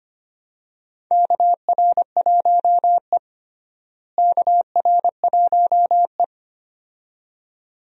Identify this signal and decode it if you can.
Morse code, heard as a keyed tone, about 25 wpm, 700 Hz. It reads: KR1E KR1E
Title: USA Call Signs - 4 Characters - Repeated 2x - 25wpm